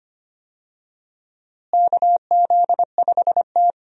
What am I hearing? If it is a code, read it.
Morse code, heard as a keyed tone, about 25 wpm, 700 Hz. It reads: KZ5T
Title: USA Call Signs - 4 Characters - Repeated 2x - 25wpm